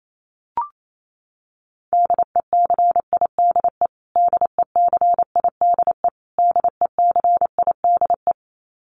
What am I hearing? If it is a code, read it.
Morse code, heard as a keyed tone, about 28 wpm, 700 Hz. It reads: DECIDE DECIDE DECIDE
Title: Top 500 Words Repeated 28wpm